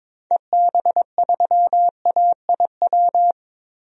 Morse code, heard as a keyed tone, about 21 wpm, 700 Hz. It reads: EB3AIW
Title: DXCC-100 Call Signs - 6 Characters - Repeated 3x - 22wpm